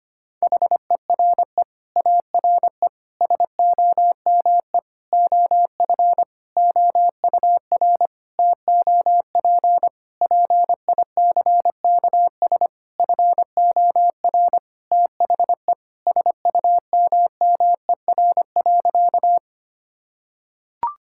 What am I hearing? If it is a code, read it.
Morse code, heard as a keyed tone, about 25 wpm, 700 Hz. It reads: HERE ARE SOME OF OUR TOP PICKS FOR THE SUMMER.
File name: practice / 25wpm / Sentences with 9-Word Phrases - 25wpm